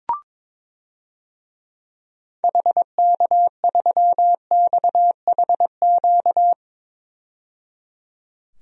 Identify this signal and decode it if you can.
Morse code, heard as a keyed tone, about 22 wpm, 700 Hz. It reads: HK3XHQ
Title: DXCC-100 Call Signs - 6 Characters - 22wpm